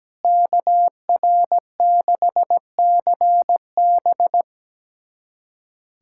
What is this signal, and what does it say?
Morse code, heard as a keyed tone, about 17 wpm, 700 Hz. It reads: KR6CB
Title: USA Callsign DE Callsign - 5 Characters - 17wpm